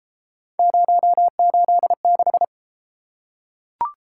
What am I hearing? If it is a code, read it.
Morse code, heard as a keyed tone, about 31 wpm, 700 Hz. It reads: 086